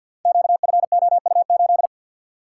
Morse code, heard as a keyed tone, about 51 wpm, 700 Hz. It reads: QFOU8
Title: Code Groups - 5 Letter:Numbers - 50wpm